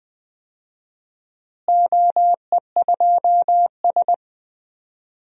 Morse code, heard as a keyed tone, about 20 wpm, 700 Hz. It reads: OE2S
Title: DXCC-100 Call Signs - 4 Characters - Repeated 3x - 20wpm